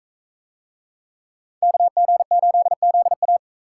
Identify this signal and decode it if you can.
Morse code, heard as a keyed tone, about 39 wpm, 700 Hz. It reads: KG8ZA